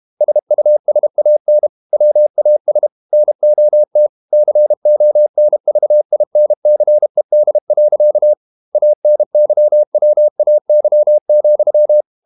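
Morse code, heard as a keyed tone, about 32 wpm, 600 Hz. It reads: SUSAN WAS NOT CONVINCED. ANYWAY,
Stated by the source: Morse code practice at 32 wpm from CW med Gnister, reading Ken Follett's novel Never in English.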